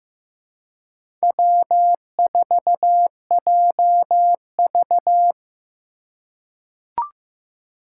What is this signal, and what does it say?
Morse code, heard as a keyed tone, about 15 wpm, 700 Hz. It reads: W4JV
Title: USA Call Signs - 4 Characters - 2x Character Spacing - 15wpm